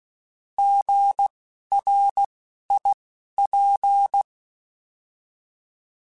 Morse code, heard as a keyed tone, about 16 wpm, spaced as for 11 wpm, 800 Hz. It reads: GRIP
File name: Alphabetical / Alphabetical mp3s / Gs @10